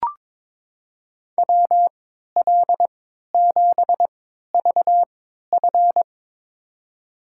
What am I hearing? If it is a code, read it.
Morse code, heard as a keyed tone, about 22 wpm, spaced as for 13 wpm, 700 Hz. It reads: WL7VF
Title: USA Call Signs - 5 Characters - 3x Character Spacing - 22wpm